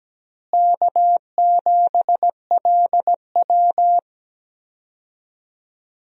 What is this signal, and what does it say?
Morse code, heard as a keyed tone, about 17 wpm, 700 Hz. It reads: K7LW
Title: USA Call Signs - 4 Characters - Repeated 3x - 17wpm